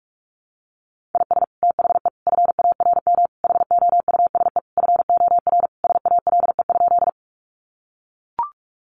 Morse code, heard as a keyed tone, about 44 wpm, 700 Hz. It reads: IS THE FARM HOUSE FOR SALE?
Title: Sentences from Top 1400 Words 45wpm